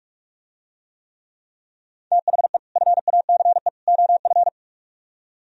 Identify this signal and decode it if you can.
Morse code, heard as a keyed tone, about 46 wpm, 700 Hz. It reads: THE FACE OF